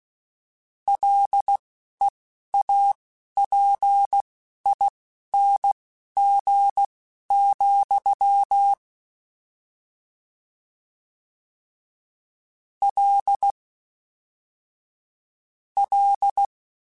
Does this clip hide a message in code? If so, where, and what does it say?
Yes: Morse code from 0.9 to 16.5 s, heard as a keyed tone, about 16 wpm, spaced as for 11 wpm, 800 Hz. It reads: LEAPING, L L